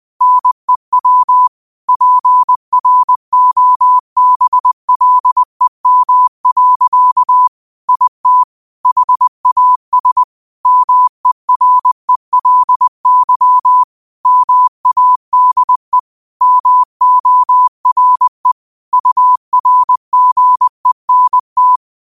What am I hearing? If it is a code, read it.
Morse code, heard as a keyed tone, about 20 wpm, 1000 Hz. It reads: NEW PROBLEM. IT HAS MERELY MADE MORE URGENT
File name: qotdfiles / 20wpm